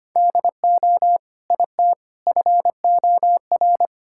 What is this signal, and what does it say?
Morse code, heard as a keyed tone, about 25 wpm, 700 Hz. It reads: DO IT FOR
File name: practice / 25wpm / Sentences with 2-Word Phrases - 25wpm